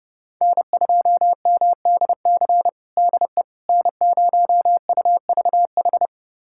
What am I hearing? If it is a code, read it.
Morse code, heard as a keyed tone, about 30 wpm, 700 Hz. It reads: N2MDC DE N0UVH
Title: USA Callsign DE Callsign - 5 Characters - 30wpm